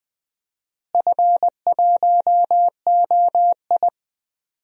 Morse code, heard as a keyed tone, about 20 wpm, 700 Hz. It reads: F1OI
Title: DXCC-100 Call Signs - 4 Characters - Repeated 2x - 20wpm